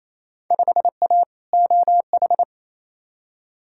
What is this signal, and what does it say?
Morse code, heard as a keyed tone, about 28 wpm, 700 Hz. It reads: 5A OH
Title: Field Day Exchanges Encore - 28wpm